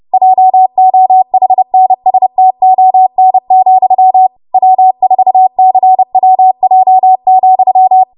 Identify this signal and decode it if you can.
Morse code, heard as a keyed tone, about 30 wpm, 750 Hz. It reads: JOHNSTON, W4CWJ,
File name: Morse / 30WPM